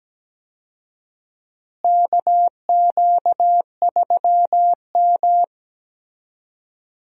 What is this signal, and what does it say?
Morse code, heard as a keyed tone, about 17 wpm, 700 Hz. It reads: KQ3M
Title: USA Call Signs - 4 Characters - Repeated 3x - 17wpm